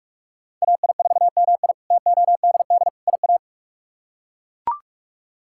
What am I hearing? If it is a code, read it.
Morse code, heard as a keyed tone, about 41 wpm, 700 Hz. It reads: AI4MI TODD IA
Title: K1USN SST QSO Practice - 45wpm